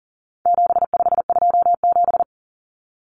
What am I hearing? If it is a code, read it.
Morse code, heard as a keyed tone, about 40 wpm, 700 Hz. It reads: 7527
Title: Numbers - 4-Digit 40wpm